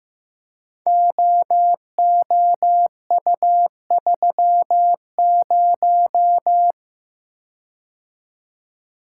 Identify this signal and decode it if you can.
Morse code, heard as a keyed tone, about 15 wpm, 700 Hz. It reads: OOU30
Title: Code Groups - 5 Letter:Numbers - 15wpm